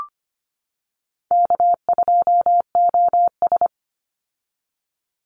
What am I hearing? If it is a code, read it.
Morse code, heard as a keyed tone, about 25 wpm, 700 Hz. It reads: K2OS
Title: Call Signs - 25wpm